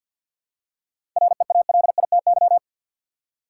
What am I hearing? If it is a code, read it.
Morse code, heard as a keyed tone, about 51 wpm, 700 Hz. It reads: REALITY
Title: Seven Letter Words - 50wpm